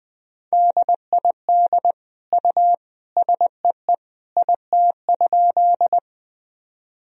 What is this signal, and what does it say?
Morse code, heard as a keyed tone, about 20 wpm, 700 Hz. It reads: DID U SEE IT?
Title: Sentences from Top 100 Words and Abbreviations 20wpm